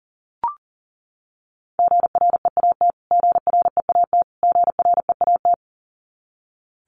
Morse code, heard as a keyed tone, about 40 wpm, 700 Hz. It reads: GREAT GREAT GREAT
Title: Top 200 Words Repeated 40wpm